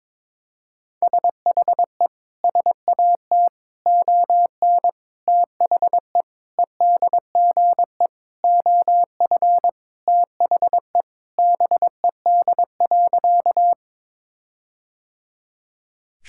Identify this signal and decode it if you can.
Morse code, heard as a keyed tone, about 22 wpm, 700 Hz. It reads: SHE SAT ON THE EDGE OF THE BED.
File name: practice / 22wpm / Sentences with 4-Word Phrases - 22wpm